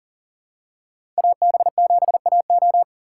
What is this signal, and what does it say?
Morse code, heard as a keyed tone, about 40 wpm, 700 Hz. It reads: AB7AO
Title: USA Call Signs - 5 Characters - 3x Character Spacing - 40wpm